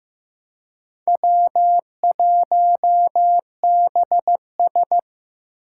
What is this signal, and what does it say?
Morse code, heard as a keyed tone, about 15 wpm, 700 Hz. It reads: W1BS
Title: Four Challenging Letter-Number Combinations 15wpm